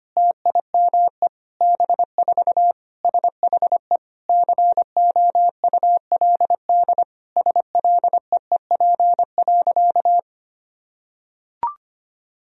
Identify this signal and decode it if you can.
Morse code, heard as a keyed tone, about 25 wpm, 700 Hz. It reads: TIME B4 SHE COULD SLEEP.
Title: Sentences from Top 500 Words and Abbreviations 25wpm